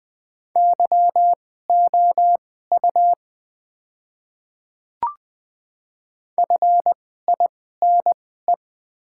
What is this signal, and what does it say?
Morse code, heard as a keyed tone, about 20 wpm, spaced as for 14 wpm, 700 Hz. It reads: YOU FINE
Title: 2x Character Spacing - Top 500 Words - 20wpm